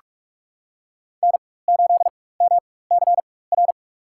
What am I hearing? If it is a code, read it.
Morse code, heard as a keyed tone, about 46 wpm, spaced as for 21 wpm, 690 Hz. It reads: N8MCR
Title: USA Call Signs - 5 Characters - 4x Character Spacing - 45wpm